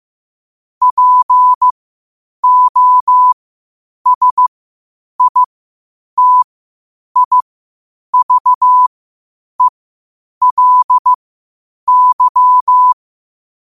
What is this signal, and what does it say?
Morse code, heard as a keyed tone, about 15 wpm, spaced as for 8 wpm, 1000 Hz. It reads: POSITIVELY
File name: qotdfiles / 8wpm